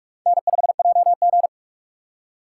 Morse code, heard as a keyed tone, about 61 wpm, spaced as for 53 wpm, 700 Hz. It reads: N5JG
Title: Speed Racing - USA 1x2 Call Signs 45wpm